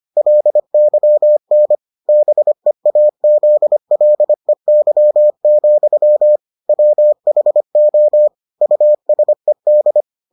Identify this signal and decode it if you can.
Morse code, heard as a keyed tone, about 25 wpm, 600 Hz. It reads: LYN BEAZLEY, WHO USED